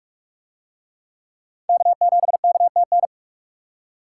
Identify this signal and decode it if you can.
Morse code, heard as a keyed tone, about 40 wpm, 700 Hz. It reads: K7KTN